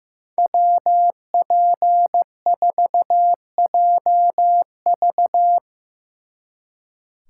Morse code, heard as a keyed tone, about 17 wpm, 700 Hz. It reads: WP4JV
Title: Speed Racing - USA Call Signs 15wpm